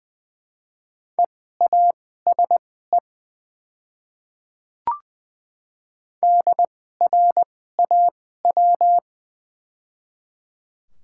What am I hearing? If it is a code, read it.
Morse code, heard as a keyed tone, about 20 wpm, spaced as for 14 wpm, 700 Hz. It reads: EASE DRAW
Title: 2x Character Spacing - Top 300 Words - 20wpm